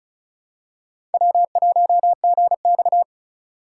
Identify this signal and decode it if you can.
Morse code, heard as a keyed tone, about 36 wpm, 700 Hz. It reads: W1GX
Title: USA Call Signs - 4 Characters - 2x Character Spacing - 35wpm